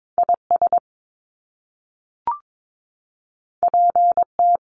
Morse code, heard as a keyed tone, about 22 wpm, 700 Hz. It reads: IS PT
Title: Top Two Letter Combinations in Words 22wpm